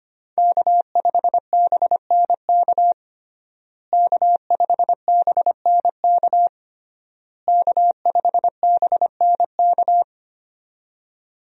Morse code, heard as a keyed tone, about 25 wpm, 700 Hz. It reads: K5BNK K5BNK K5BNK
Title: USA Call Signs - 5 Characters - Repeated 3x - 25wpm